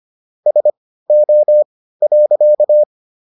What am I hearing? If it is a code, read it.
Morse code, heard as a keyed tone, about 25 wpm, spaced as for 15 wpm, 600 Hz. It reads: SO.